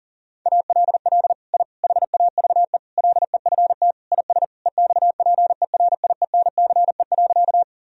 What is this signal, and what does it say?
Morse code, heard as a keyed tone, about 40 wpm, 700 Hz. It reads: ALL I HAVE LEFT IS EXPERIENCE.
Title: Sentences from Top 1000 Words - Review Part 3 - 40wpm